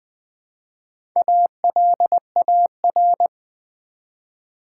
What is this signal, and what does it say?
Morse code, heard as a keyed tone, about 20 wpm, 700 Hz. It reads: ALAR
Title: Build A Word - 1 Letter At A Time - 5-Letter Words - 20wpm